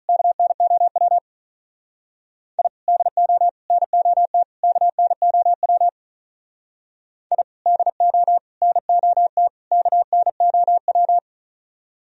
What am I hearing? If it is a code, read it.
Morse code, heard as a keyed tone, about 43 wpm, 700 Hz. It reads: KNOW I DO NOT KNOW I DO NOT KNOW
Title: Speed-Racing - Sets of 4 Words 35wpm